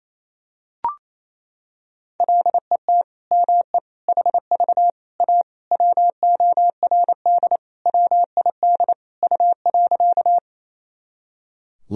Morse code, heard as a keyed tone, about 28 wpm, 700 Hz. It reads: LET ME HV A WORD WID U.
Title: Sentences from Top 300 Words and Abbreviations 28wpm